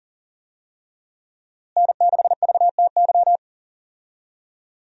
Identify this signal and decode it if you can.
Morse code, heard as a keyed tone, about 40 wpm, 700 Hz. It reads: N6VTY
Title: USA Call Signs - 5 Characters - Repeated 3x - 40wpm